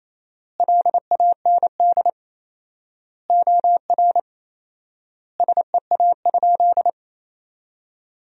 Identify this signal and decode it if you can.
Morse code, heard as a keyed tone, about 28 wpm, 700 Hz. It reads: LAND OR SEA?
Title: 4x Word-Spacing - Sentences from Top 500 Words - 28wpm